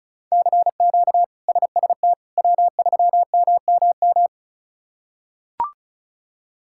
Morse code, heard as a keyed tone, about 36 wpm, 700 Hz. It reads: CQ SST W3MMM
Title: K1USN SST QSO Practice - 35wpm